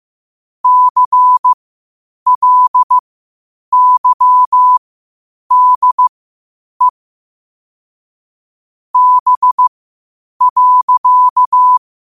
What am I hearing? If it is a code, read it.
Morse code, heard as a keyed tone, about 15 wpm, spaced as for 8 wpm, 1000 Hz. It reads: CLYDE B.
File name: qotdfiles / 8wpm